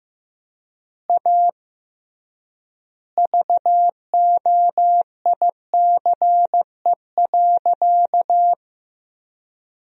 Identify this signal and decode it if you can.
Morse code, heard as a keyed tone, about 15 wpm, 700 Hz. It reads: A VOICE.
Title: 3x Word-Spacing - Sentences from Top 500 Words - 15wpm